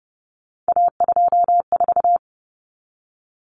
Morse code, heard as a keyed tone, about 30 wpm, 700 Hz. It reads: A24
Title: Common Cut Numbers - 3-Digit 30wpm